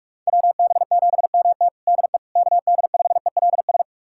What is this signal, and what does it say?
Morse code, heard as a keyed tone, about 46 wpm, 700 Hz. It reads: WB7MT DE KD5ELS
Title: USA Callsign DE Callsign 45wpm